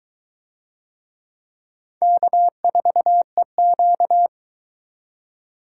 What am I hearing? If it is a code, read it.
Morse code, heard as a keyed tone, about 22 wpm, 700 Hz. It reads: K4EQ